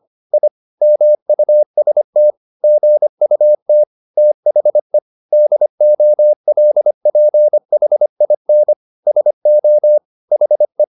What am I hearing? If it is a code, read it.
Morse code, heard as a keyed tone, about 25 wpm, 600 Hz. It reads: I MUST GUT THE DOLPHIN SO HE